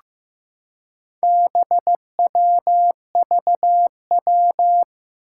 Morse code, heard as a keyed tone, about 15 wpm, 700 Hz. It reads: BWVW
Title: Four Challenging Letter-Number Combinations 15wpm